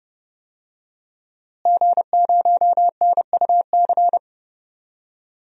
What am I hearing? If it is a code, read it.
Morse code, heard as a keyed tone, about 30 wpm, 700 Hz. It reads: G0NUC
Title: DXCC-100 Call Signs - 5 Characters - Repeated 2x - 30wpm